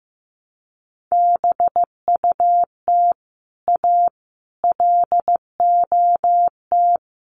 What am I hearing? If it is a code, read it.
Morse code, heard as a keyed tone, about 15 wpm, 700 Hz. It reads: BUT A LOT
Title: Sets of 5 Words Encore2 15wpm